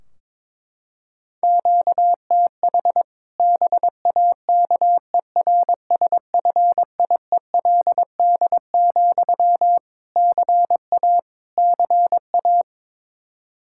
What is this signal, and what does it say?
Morse code, heard as a keyed tone, about 22 wpm, 700 Hz. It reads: QTH BAKERSFIELD, CA CA
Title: Practice QSO - Exchange 1 - Beginning - 22wpm